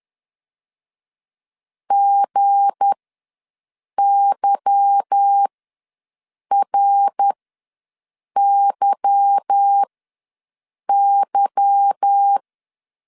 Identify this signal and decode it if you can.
Morse code, heard as a keyed tone, about 10 wpm, spaced as for 6 wpm, 800 Hz. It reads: GYRYY